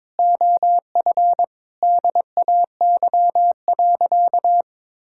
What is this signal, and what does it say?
Morse code, heard as a keyed tone, about 22 wpm, 700 Hz. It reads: OF DAY.
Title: Sentences from Top 1000 Words - Review Part 5 - 22wpm